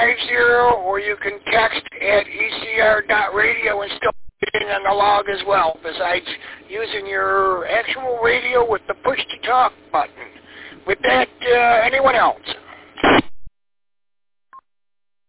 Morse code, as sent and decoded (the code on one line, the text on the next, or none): none